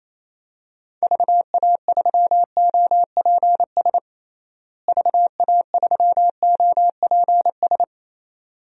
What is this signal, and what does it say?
Morse code, heard as a keyed tone, about 28 wpm, 700 Hz. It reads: VA3OPS VA3OPS
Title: DXCC-100 Call Signs - 6 Characters - Repeated 2x - 28wpm